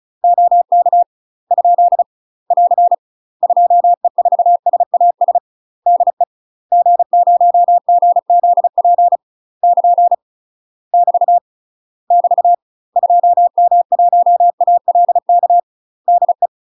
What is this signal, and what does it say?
Morse code, heard as a keyed tone, about 36 wpm, 700 Hz. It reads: OK ? <AR> 2E4SAS DE G0GZP <KN> <BT> <BT> 2M1ALK DE
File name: UK MASTER QSOs TRAINING / 35WPM / UK MASTER QSOs 0032